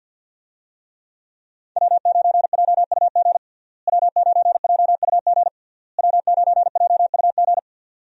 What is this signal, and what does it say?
Morse code, heard as a keyed tone, about 51 wpm, 700 Hz. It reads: W9JUG W9JUG W9JUG
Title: USA Call Signs - 5 Characters - Repeated 3x - 50wpm